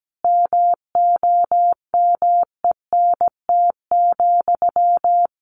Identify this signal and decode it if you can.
Morse code, heard as a keyed tone, about 17 wpm, 700 Hz. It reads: MOMENT,
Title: Sentences from Top 1000 Words 17wpm